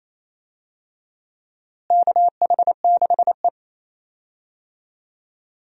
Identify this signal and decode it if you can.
Morse code, heard as a keyed tone, about 28 wpm, 700 Hz. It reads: KH6E
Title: DXCC-100 Call Signs - 4 Characters - Repeated 2x - 28wpm